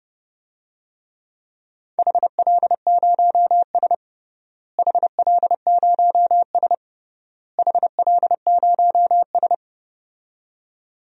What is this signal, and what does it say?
Morse code, heard as a keyed tone, about 30 wpm, 700 Hz. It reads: HL0S HL0S HL0S
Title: DXCC-100 Call Signs - 4 Characters - Repeated 3x - 30wpm